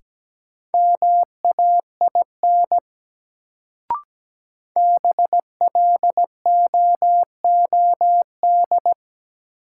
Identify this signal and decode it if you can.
Morse code, heard as a keyed tone, about 17 wpm, 700 Hz. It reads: MAIN BLOOD
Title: Sentences from Top 1000 Words - Review Part 2 - New Word Drill - 17wpm